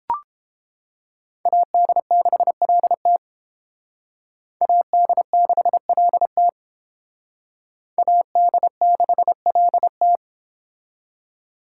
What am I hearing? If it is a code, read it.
Morse code, heard as a keyed tone, about 30 wpm, 700 Hz. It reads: AD6LT AD6LT AD6LT